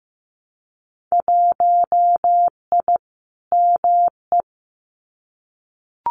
Morse code, heard as a keyed tone, about 15 wpm, 700 Hz. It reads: 1I ME
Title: Winter Field Day Exchanges 15wpm